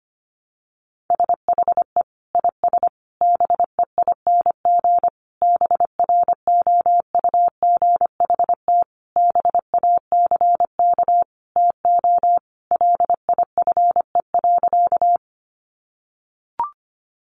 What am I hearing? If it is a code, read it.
Morse code, heard as a keyed tone, about 25 wpm, 700 Hz. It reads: SHE IS BEING BROUGHT BACK TO LIFE.
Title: Sentences from Top 600 Words 25wpm